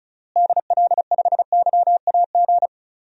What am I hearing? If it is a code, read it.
Morse code, heard as a keyed tone, about 36 wpm, 700 Hz. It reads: DL5YAG
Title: DXCC-100 Call Signs - 6 Characters - Repeated 3x - 35wpm